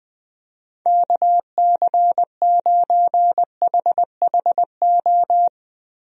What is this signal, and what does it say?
Morse code, heard as a keyed tone, about 20 wpm, 700 Hz. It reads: KC9HHO
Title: USA Call Signs - 6 Characters - Repeated 2x - 20wpm